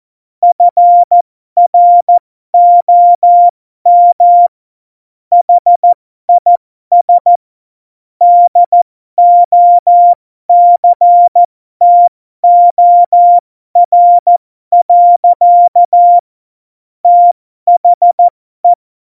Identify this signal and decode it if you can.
Morse code, heard as a keyed tone, about 14 wpm, spaced as for 12 wpm, 700 Hz. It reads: FROM HIS DOCTOR. THE
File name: Morsekurs / Witze